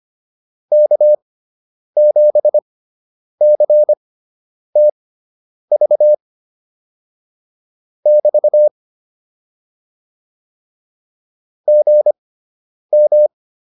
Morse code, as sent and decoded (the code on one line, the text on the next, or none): -.- --... -.-. - ...- / -...- / --. --
K7CTV <BT> GM